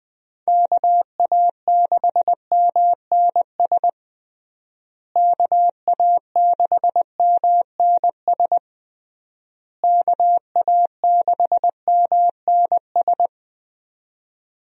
Morse code, heard as a keyed tone, about 21 wpm, 700 Hz. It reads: KA6MNS KA6MNS KA6MNS
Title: USA Call Signs - 6 Characters - Repeated 3x - 20wpm